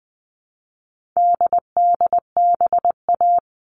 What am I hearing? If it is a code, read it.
Morse code, heard as a keyed tone, about 20 wpm, 700 Hz. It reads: DDBA
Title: All Cut Numbers - 4-Digit 20wpm